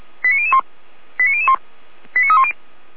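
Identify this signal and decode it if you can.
Five-tone selective call (ZVEI), heard as quick runs of five tones, tones of about 70 ms. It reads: 89001 89001 89310
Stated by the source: ZVEI five-tone selective call 89167 received on SDR by feuerwehr-orlamuende (fire brigade Orlamünde).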